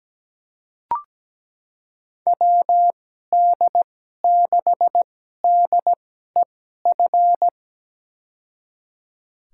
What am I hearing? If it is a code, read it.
Morse code, heard as a keyed tone, about 17 wpm, spaced as for 12 wpm, 700 Hz. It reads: WD6DEF
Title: USA Call Signs - 6 Characters - 2x Character Spacing - 17wpm